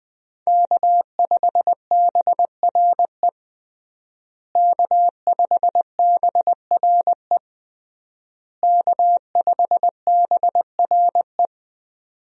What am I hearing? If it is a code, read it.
Morse code, heard as a keyed tone, about 20 wpm, 700 Hz. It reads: K5BRE K5BRE K5BRE
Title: USA Call Signs - 5 Characters - Repeated 3x - 20wpm